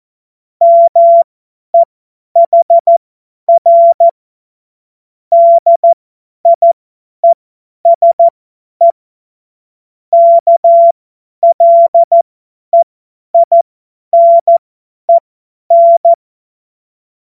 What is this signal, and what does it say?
Morse code, heard as a keyed tone, about 14 wpm, spaced as for 10 wpm, 680 Hz. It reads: MEHR DIESE KLEINEN